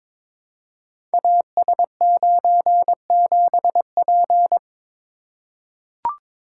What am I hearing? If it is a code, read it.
Morse code, heard as a keyed tone, about 22 wpm, 700 Hz. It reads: AS97P
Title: Code Groups - 5 Letter:Numbers - 22wpm